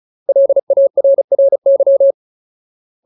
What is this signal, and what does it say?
Morse code, heard as a keyed tone, about 34 wpm, 550 Hz. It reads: LARRY
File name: Ultimate Word List / 35WPM Ultimate